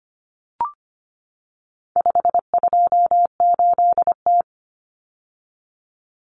Morse code, heard as a keyed tone, about 25 wpm, 700 Hz. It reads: 528T
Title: Common Cut Numbers - 4-Digit 25wpm